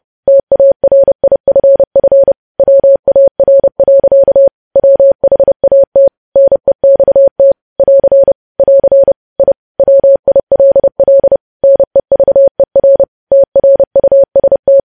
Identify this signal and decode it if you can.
Morse code, heard as a keyed tone, about 30 wpm, 550 Hz. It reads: TARIFF WAR. WHAT NEXT <AR> <AR> I WILL NEVER TRUST
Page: news headlines in morse code at 30 wpm